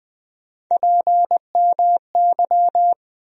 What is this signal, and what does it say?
Morse code, heard as a keyed tone, about 21 wpm, 700 Hz. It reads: PMY
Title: USA Call Sign Postfixes - 3 Letters 20wpm